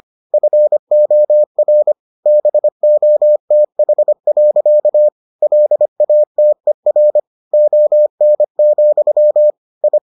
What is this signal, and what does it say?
Morse code, heard as a keyed tone, about 25 wpm, 600 Hz. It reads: FOR BOTH. LATER ON, I